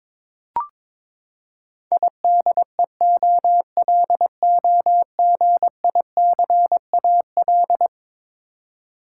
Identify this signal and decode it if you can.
Morse code, heard as a keyed tone, about 22 wpm, 700 Hz. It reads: IDEOLOGICAL
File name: practice / 22wpm / Eleven Letter Words - 22wpm